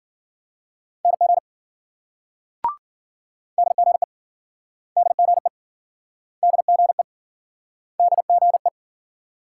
Morse code, heard as a keyed tone, about 53 wpm, 700 Hz. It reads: NG DGE DGE DGE DGE